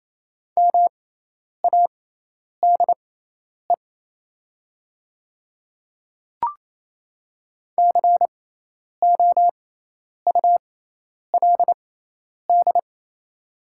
Morse code, heard as a keyed tone, about 28 wpm, spaced as for 10 wpm, 690 Hz. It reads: MADE COULD
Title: 6x Character Spacing - Top 200 Words - 28wpm